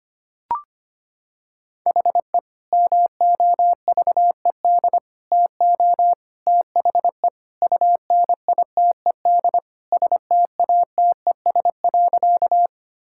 Morse code, heard as a keyed tone, about 25 wpm, 700 Hz. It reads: HE MOVED TO THE UNITED STATES.
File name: practice / 25wpm / Sentences with 3-Word Phrases - 25wpm